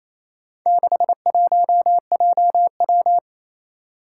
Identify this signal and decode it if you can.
Morse code, heard as a keyed tone, about 28 wpm, 700 Hz. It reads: 61JW
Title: Four Challenging Letter-Number Combinations 28wpm